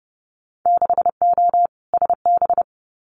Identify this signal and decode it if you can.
Morse code, heard as a keyed tone, about 32 wpm, 700 Hz. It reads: 6O SB
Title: Winter Field Day Exchanges Encore 30wpm